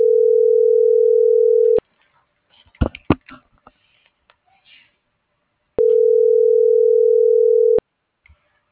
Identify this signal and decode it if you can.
Ringback tone (North America), 440 + 480 Hz.